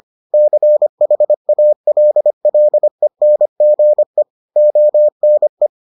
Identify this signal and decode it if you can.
Morse code, heard as a keyed tone, about 25 wpm, 600 Hz. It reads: CHALLENGE ONE